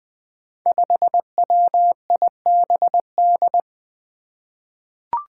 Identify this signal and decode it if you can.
Morse code, heard as a keyed tone, about 20 wpm, 700 Hz. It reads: 5WIBD
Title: Five Challenging Letter-Number Combinations 20wpm